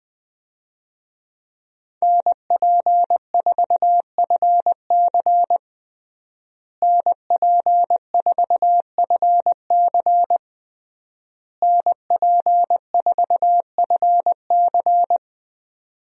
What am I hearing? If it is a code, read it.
Morse code, heard as a keyed tone, about 21 wpm, 700 Hz. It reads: NP4FC NP4FC NP4FC
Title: USA Call Signs - 5 Characters - Repeated 3x - 20wpm